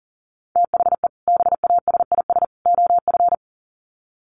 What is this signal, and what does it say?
Morse code, heard as a keyed tone, about 37 wpm, 700 Hz. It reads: THE BASIS OF
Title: Sets of 3 Words 40wpm